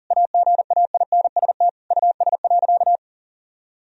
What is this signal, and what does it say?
Morse code, heard as a keyed tone, about 40 wpm, 700 Hz. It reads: AGAINST US.
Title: Sentences from Top 1000 Words - Review Part 2 - 40wpm